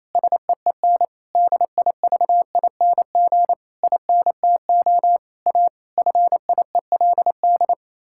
Morse code, heard as a keyed tone, about 28 wpm, 700 Hz. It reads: SEEN DIVING INTO A FIELD
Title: Pangram Challenge v2 - 28wpm